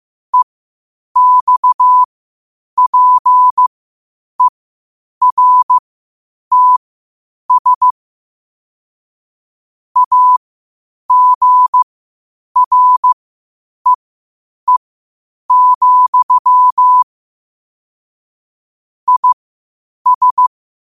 Morse code, heard as a keyed tone, about 15 wpm, spaced as for 8 wpm, 1000 Hz. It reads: EXPERTS AGREE, IS
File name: qotdfiles / 8wpm